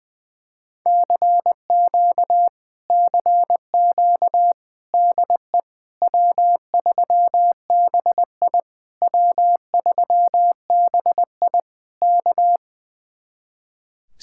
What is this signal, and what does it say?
Morse code, heard as a keyed tone, about 20 wpm, 700 Hz. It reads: CQ CQ DE W3BI W3BI K
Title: Speed-Racing Common QSO Phrases - 20wpm